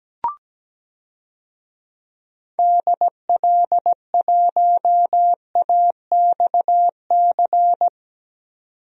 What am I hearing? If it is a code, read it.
Morse code, heard as a keyed tone, about 17 wpm, 700 Hz. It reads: DL1AXC
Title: DXCC-100 Call Signs - 6 Characters - 17wpm